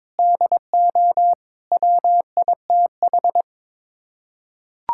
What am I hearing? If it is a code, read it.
Morse code, heard as a keyed tone, about 22 wpm, 700 Hz. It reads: DO WITH
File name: Sets of 9 Words - 22wpm